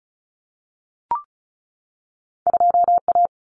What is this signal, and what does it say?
Morse code, heard as a keyed tone, about 36 wpm, 700 Hz. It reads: 2A
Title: Common Cut Numbers - 2-Digit 35wpm